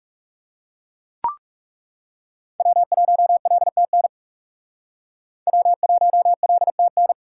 Morse code, heard as a keyed tone, about 38 wpm, 700 Hz. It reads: W1LTN W1LTN